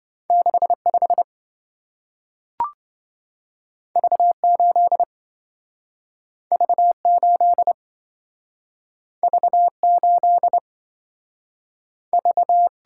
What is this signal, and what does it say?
Morse code, heard as a keyed tone, about 27 wpm, 700 Hz. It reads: 65 V8 V8 V8 V